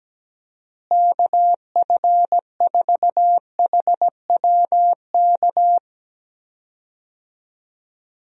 Morse code, heard as a keyed tone, about 17 wpm, 700 Hz. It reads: KF4HWK
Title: USA Call Signs - 6 Characters - Repeated 2x - 17wpm